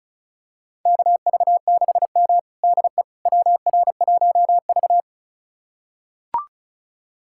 Morse code, heard as a keyed tone, about 36 wpm, 700 Hz. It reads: KV6M DE WR1V
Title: USA Callsign DE Callsign - 4 Characters - 35wpm